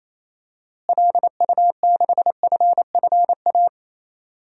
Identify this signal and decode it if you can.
Morse code, heard as a keyed tone, about 28 wpm, 700 Hz. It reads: LU6FFA
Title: DXCC-100 Call Signs - 6 Characters - Repeated 3x - 28wpm